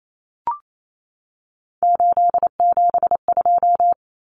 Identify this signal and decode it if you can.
Morse code, heard as a keyed tone, about 28 wpm, 700 Hz. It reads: 872
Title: Common Cut Numbers - 3-Digit 28wpm